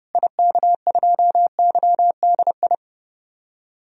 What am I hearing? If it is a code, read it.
Morse code, heard as a keyed tone, about 32 wpm, 700 Hz. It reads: IK2YDI
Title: DXCC-100 Call Signs - 6 Characters - Repeated 3x - 30wpm